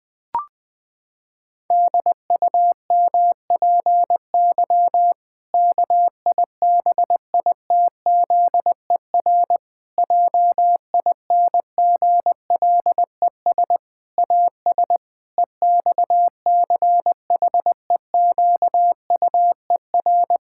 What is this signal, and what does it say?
Morse code, heard as a keyed tone, about 21 wpm, 700 Hz. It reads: DUMPY KIBITZER JINGLES AS EXCHEQUER
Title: Pangram Challenge v2 - 20wpm